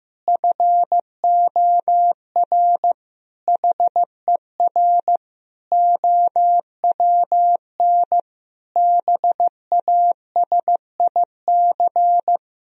Morse code, heard as a keyed tone, about 15 wpm, 700 Hz. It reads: FOR HER OWN BASIC
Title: Sentences from Top 1000 Words - Review Part 3 - 15wpm